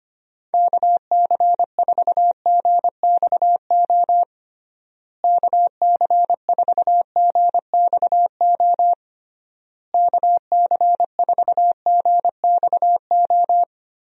Morse code, heard as a keyed tone, about 25 wpm, 700 Hz. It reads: KC4GXO KC4GXO KC4GXO
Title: USA Call Signs - 6 Characters - Repeated 3x - 25wpm